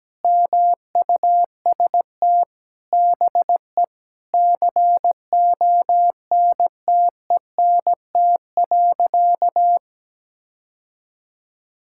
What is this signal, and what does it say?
Morse code, heard as a keyed tone, about 17 wpm, 700 Hz. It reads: MUST BE CONTENT.
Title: Sentences from Top 1000 Words - Review Part 6 - 17wpm